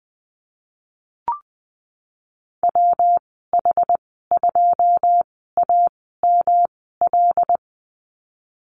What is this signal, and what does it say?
Morse code, heard as a keyed tone, about 20 wpm, spaced as for 15 wpm, 700 Hz. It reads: WH2AML
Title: 122 Lesson 20g - Number 3 - Call Signs 20wpm